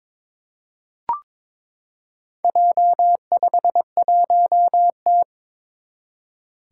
Morse code, heard as a keyed tone, about 22 wpm, 700 Hz. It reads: J51T
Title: Four Challenging Letter-Number Combinations 22wpm